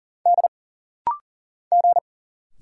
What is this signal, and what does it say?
Morse code, heard as a keyed tone, about 40 wpm, 700 Hz. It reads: D G
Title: Single Letters - Rapid-Fire 40wpm